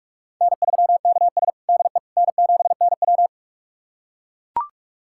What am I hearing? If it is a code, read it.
Morse code, heard as a keyed tone, about 41 wpm, 700 Hz. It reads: N3KS DE N7NW
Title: USA Callsign DE Callsign - 4 Characters - 45wpm